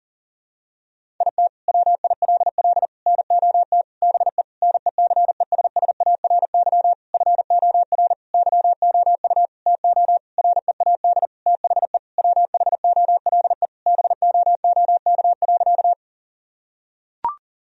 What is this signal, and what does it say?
Morse code, heard as a keyed tone, about 40 wpm, 700 Hz. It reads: IT WILL NOT BE NECESSARY FOR YOU TO READ THE WHOLE BOOK.
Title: Sentences from Top 1000 Words - Review Part 1 - 40wpm